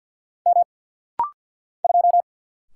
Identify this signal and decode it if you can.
Morse code, heard as a keyed tone, about 50 wpm, 700 Hz. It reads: M 2
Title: Single Letter-Number - Rapid-Fire 50wpm